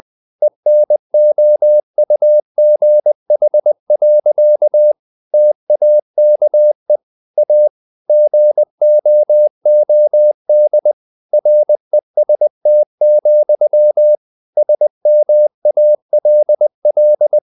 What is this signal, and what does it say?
Morse code, heard as a keyed tone, about 20 wpm, 600 Hz. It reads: ENOUGH. TAKE A GOOD REST, SMALL
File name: the old man and the sea / the old man and the sea part 1 20wpm